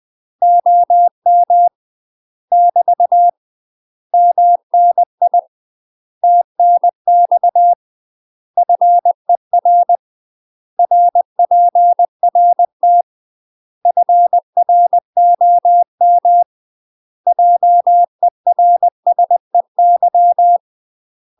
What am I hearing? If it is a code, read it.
Morse code, heard as a keyed tone, about 20 wpm, 700 Hz. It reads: OM <BT> MNI TNX FER RPRT FROM JERSEY
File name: UK MASTER QSOs TRAINING / 20WPM / UK MASTER QSOs 0011